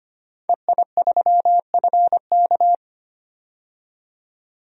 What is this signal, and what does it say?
Morse code, heard as a keyed tone, about 25 wpm, 700 Hz. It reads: EI3FK
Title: DXCC-100 Call Signs - 5 Characters - 25wpm